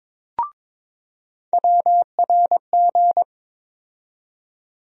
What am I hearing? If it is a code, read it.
Morse code, heard as a keyed tone, about 22 wpm, 700 Hz. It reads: WRG
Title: USA Call Sign Postfixes - 3 Letters 22wpm